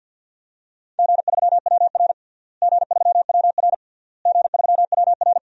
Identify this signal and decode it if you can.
Morse code, heard as a keyed tone, about 50 wpm, 700 Hz. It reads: G3WR G3WR G3WR
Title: DXCC-100 Call Signs - 4 Characters - Repeated 3x - 50wpm